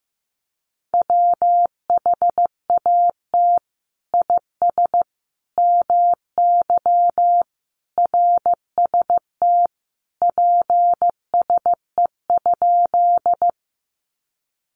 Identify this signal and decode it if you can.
Morse code, heard as a keyed tone, about 15 wpm, 700 Hz. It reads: WHAT IS MY RST PSE?
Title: Common QSO Phrases 15wpm